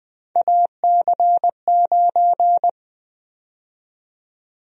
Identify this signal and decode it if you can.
Morse code, heard as a keyed tone, about 20 wpm, 700 Hz. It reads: AC9
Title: USA Call Sign Prefixes - 20wpm